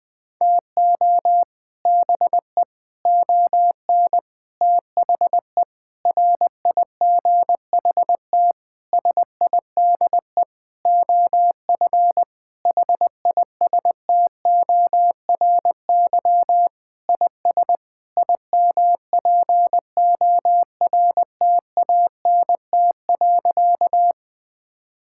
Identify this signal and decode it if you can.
Morse code, heard as a keyed tone, about 21 wpm, 700 Hz. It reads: TO BE ON THE RIGHT SIDE OF HISTORY IS IMPORTANT.
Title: Sentences with 8-Word Phrases - 20wpm